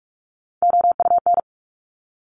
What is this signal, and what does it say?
Morse code, heard as a keyed tone, about 46 wpm, 700 Hz. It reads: OUN
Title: Top Three Letter Combinations in Words 45wpm